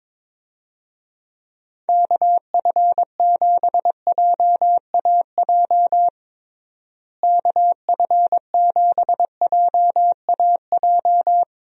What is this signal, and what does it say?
Morse code, heard as a keyed tone, about 22 wpm, 700 Hz. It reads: KF7JAJ KF7JAJ
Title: USA Call Signs - 6 Characters - Repeated 2x - 22wpm